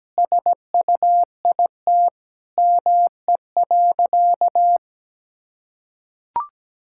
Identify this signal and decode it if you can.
Morse code, heard as a keyed tone, about 17 wpm, 700 Hz. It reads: SUIT ME.
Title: Sentences from Top 1600 Words 17wpm